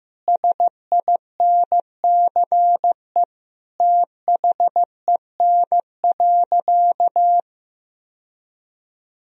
Morse code, heard as a keyed tone, about 15 wpm, 700 Hz. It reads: SINCE THEN.
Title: Sentences from Top 1000 Words - Review Part 1 - 15wpm